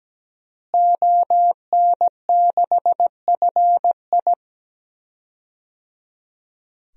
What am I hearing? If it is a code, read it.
Morse code, heard as a keyed tone, about 17 wpm, 700 Hz. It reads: ON6FI
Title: DXCC-100 Call Signs - 5 Characters - Repeated 3x - 17wpm